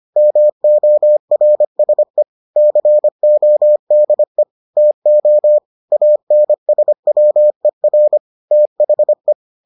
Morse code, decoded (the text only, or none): MORSE CODE TO ANSWER THE